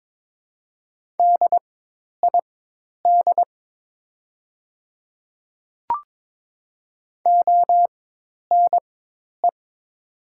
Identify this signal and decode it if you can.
Morse code, heard as a keyed tone, about 22 wpm, spaced as for 10 wpm, 700 Hz. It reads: DID ONE
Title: 4x Character Spacing - Top 100 Words - 22wpm